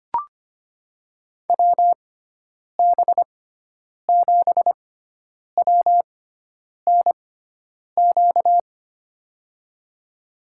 Morse code, heard as a keyed tone, about 25 wpm, spaced as for 9 wpm, 700 Hz. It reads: WB7WNQ
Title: USA Call Signs - 6 Characters - 6x Character Spacing - 25wpm